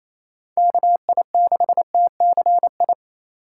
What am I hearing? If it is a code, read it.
Morse code, heard as a keyed tone, about 27 wpm, 700 Hz. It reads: KI6TCI
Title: USA Call Signs - 2x Character Spacing - 28wpm